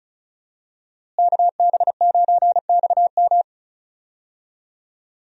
Morse code, heard as a keyed tone, about 36 wpm, 700 Hz. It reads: KB9XM
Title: USA Call Signs - 5 Characters - Repeated 3x - 35wpm